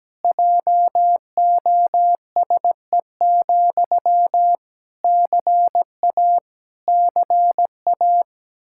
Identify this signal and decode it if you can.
Morse code, heard as a keyed tone, about 17 wpm, 700 Hz. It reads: JOSE, CA CA
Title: Practice QSO - Exchange 1 - Beginning - 17wpm